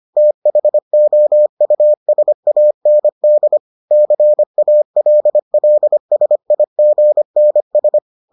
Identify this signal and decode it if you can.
Morse code, heard as a keyed tone, about 25 wpm, 600 Hz. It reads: THOUSAND CALLSIGNS